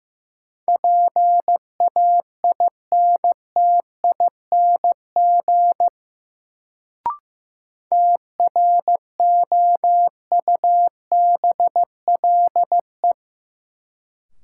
Morse code, heard as a keyed tone, about 15 wpm, 700 Hz. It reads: PAINTING TROUBLE
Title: Sentences from Top 1200 Words - New Word Drill Alt - 15wpm